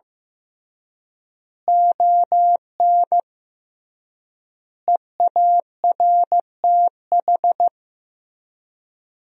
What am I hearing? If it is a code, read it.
Morse code, heard as a keyed tone, about 15 wpm, 700 Hz. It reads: ON EARTH
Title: 3x Word-Spacing - Sentences from Top 200 Words - 15wpm